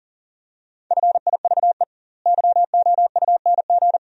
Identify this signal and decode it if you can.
Morse code, heard as a keyed tone, about 40 wpm, 700 Hz. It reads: FIVE YOUNG
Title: 2x Word-Spacing - Sentences from Top 400 Words - 40wpm